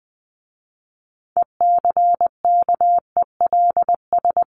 Morse code, heard as a keyed tone, about 20 wpm, 700 Hz. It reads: ECKELS
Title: A Sound Of Thunder 20wpm